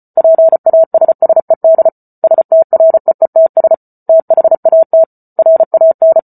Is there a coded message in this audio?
Yes: Morse code from 0.2 to 6.2 s, heard as a keyed tone, about 34 wpm, 650 Hz. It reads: PASSED STREETS THAT RAN